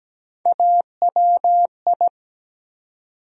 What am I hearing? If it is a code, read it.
Morse code, heard as a keyed tone, about 17 wpm, 700 Hz. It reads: AWI
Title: USA Call Sign Postfixes - 3 Letters 17wpm